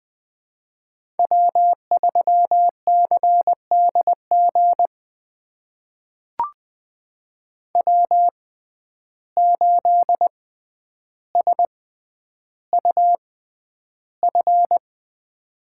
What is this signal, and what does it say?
Morse code, heard as a keyed tone, about 20 wpm, spaced as for 10 wpm, 700 Hz. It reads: W3CDG W8SUF
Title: USA Call Signs - 5 Characters - 6x Character Spacing - 20wpm